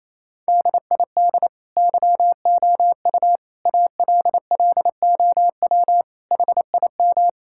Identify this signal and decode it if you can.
Morse code, heard as a keyed tone, about 28 wpm, 700 Hz. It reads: DID YOU ALLOW HIM
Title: Sentences from Top 1000 Words - Review Part 5 - 28wpm